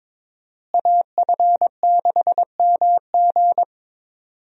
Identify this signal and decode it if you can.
Morse code, heard as a keyed tone, about 22 wpm, 700 Hz. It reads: AF6MG
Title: USA Call Signs - 5 Characters - Repeated 3x - 22wpm